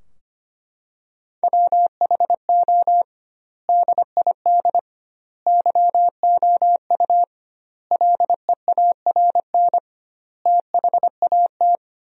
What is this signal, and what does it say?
Morse code, heard as a keyed tone, about 25 wpm, 700 Hz. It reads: WHO DID YOU LEARN THAT
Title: 2x Word-Spacing - Sentences from Top 300 Words - 25wpm